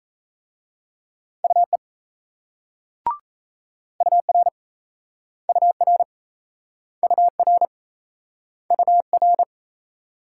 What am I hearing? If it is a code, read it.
Morse code, heard as a keyed tone, about 38 wpm, 700 Hz. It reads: UE UR UR UR UR